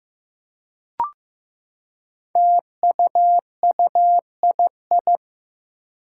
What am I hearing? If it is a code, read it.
Morse code, heard as a keyed tone, about 15 wpm, 700 Hz. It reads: TUUII
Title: Five Challenging Letter-Number Combinations 15wpm